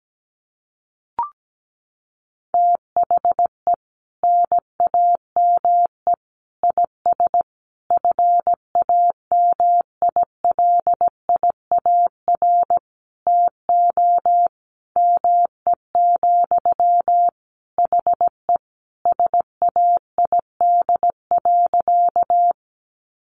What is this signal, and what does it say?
Morse code, heard as a keyed tone, about 17 wpm, 700 Hz. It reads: THE NAME IS FAMILIAR TO ME, HE SAID.
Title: Sentences from Top 1000 Words 17wpm